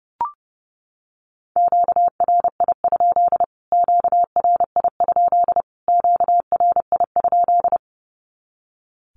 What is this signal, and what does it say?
Morse code, heard as a keyed tone, about 30 wpm, 700 Hz. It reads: QRI? QRI? QRI?
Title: Q-Codes Repeated 3x 30wpm